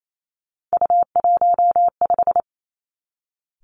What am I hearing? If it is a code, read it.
Morse code, heard as a keyed tone, about 28 wpm, 700 Hz. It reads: U15
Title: Three Challenging Letter-Number Combinations 28wpm